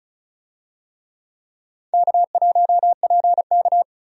Morse code, heard as a keyed tone, about 36 wpm, 700 Hz. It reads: K1PK
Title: USA Call Signs - 4 Characters - 35wpm